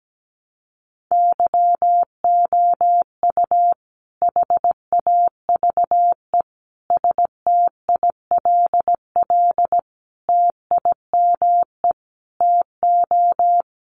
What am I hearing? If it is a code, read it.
Morse code, heard as a keyed tone, about 17 wpm, 700 Hz. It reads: YOU HAVE STILL TIME TO
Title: Sentences from Top 600 Words 17wpm